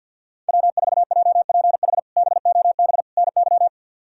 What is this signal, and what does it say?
Morse code, heard as a keyed tone, about 48 wpm, 700 Hz. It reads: W4JPH BOB NY
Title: K1USN SST QSO Practice - 50wpm